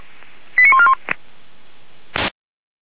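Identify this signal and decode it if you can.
Five-tone selective call (ZVEI), heard as quick runs of five tones, tones of about 80 ms. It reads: 89141
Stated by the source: ZVEI five-tone call received by SDR, archived by feuerwehr-orlamuende.org (fire brigade Orlamünde).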